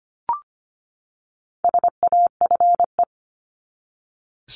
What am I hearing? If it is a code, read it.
Morse code, heard as a keyed tone, about 25 wpm, 700 Hz. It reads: SAFE
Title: Top 751-1000 Words 25wpm